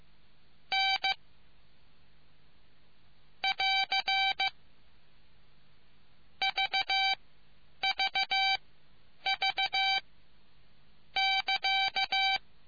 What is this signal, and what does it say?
Morse code, heard as a keyed tone, about 15 wpm, spaced as for 9 wpm, 2300 Hz. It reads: N <AR> VVV <KA>